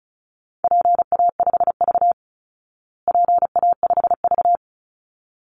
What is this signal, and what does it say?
Morse code, heard as a keyed tone, about 34 wpm, 700 Hz. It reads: PA5V PA5V
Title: Call Signs Repeated 3x - 35wpm